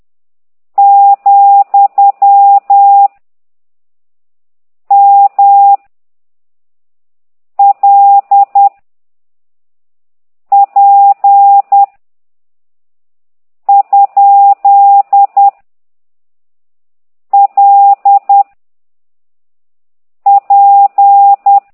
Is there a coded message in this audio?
Yes: Morse code from 0.8 to 21.6 s, heard as a keyed tone, about 10 wpm, 800 Hz. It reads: , M L P ? L P